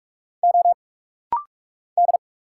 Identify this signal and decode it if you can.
Morse code, heard as a keyed tone, about 46 wpm, 700 Hz. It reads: O D
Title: Single Letters - Rapid-Fire 45wpm